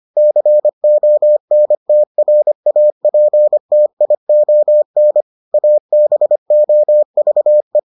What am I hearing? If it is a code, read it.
Morse code, heard as a keyed tone, about 25 wpm, 600 Hz. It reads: CONTRAPTION ABOVE